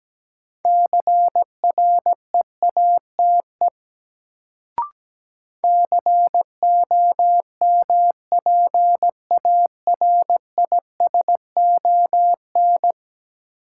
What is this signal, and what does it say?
Morse code, heard as a keyed tone, about 17 wpm, 700 Hz. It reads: CREATE COMPARISON
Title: Sentences from Top 1600 Words - New Word Drill Alt - 17wpm